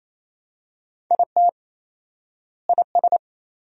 Morse code, heard as a keyed tone, about 28 wpm, 700 Hz. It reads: IT IS
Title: 4x Word-Spacing - Sentences from Top 400 Words - 28wpm